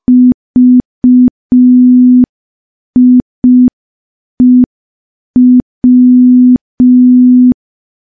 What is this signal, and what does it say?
Morse code, heard as a keyed tone, about 5 wpm, 250 Hz. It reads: VIEW
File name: morse / morse-training-05WPM